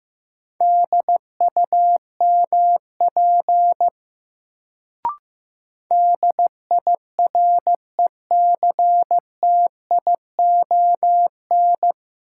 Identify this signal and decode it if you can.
Morse code, heard as a keyed tone, about 15 wpm, 700 Hz. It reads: DUMP DIRECTION
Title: Sentences from Top 1300 Words - New Word Drill Alt - 15wpm